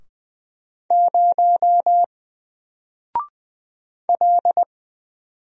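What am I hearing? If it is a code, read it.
Morse code, heard as a keyed tone, about 20 wpm, 700 Hz. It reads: 0 L
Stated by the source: Single letter and number Morse drill at 20 wpm, from N4LCD.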